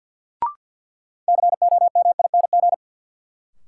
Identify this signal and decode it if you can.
Morse code, heard as a keyed tone, about 52 wpm, 700 Hz. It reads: COMING
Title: Sentences from Top 1300 Words - New Word Drill Alt - 50wpm